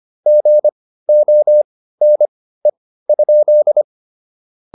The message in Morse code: GONE?

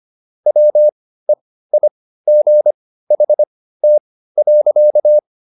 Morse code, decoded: WEIGHT.

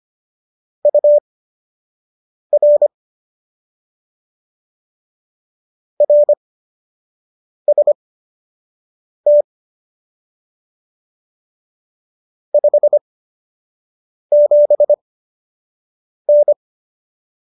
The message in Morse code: UR RST 57N

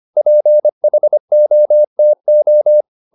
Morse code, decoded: PHOTO

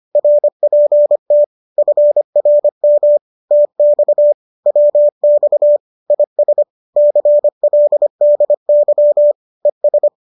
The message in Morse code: RPT FRM TX WX IS CLDY ES

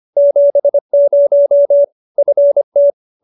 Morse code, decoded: 70 FT

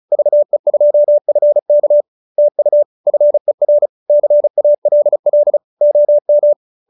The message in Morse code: VE2FK TU FER CALL OM

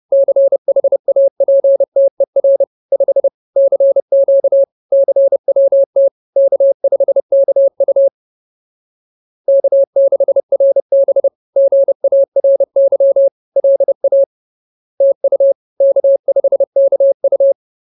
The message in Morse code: CHAPTER 5 CQ CWT K5KU K6RB GARY LA TU K5KU

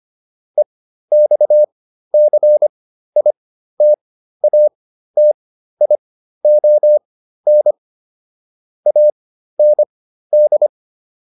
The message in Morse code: EXCITATION AND

